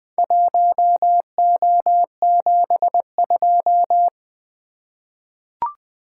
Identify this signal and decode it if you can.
Morse code, heard as a keyed tone, about 20 wpm, 700 Hz. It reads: 1O72
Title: Four Letter-Number Combinations - 20wpm